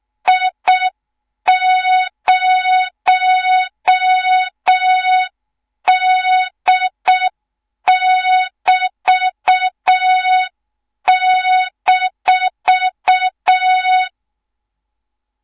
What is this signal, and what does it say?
Morse code, heard as a keyed tone, about 6 wpm, 3700 Hz. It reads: I0D<BT>-
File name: Morze-HA2MN / hang / 24 GYAKORLAT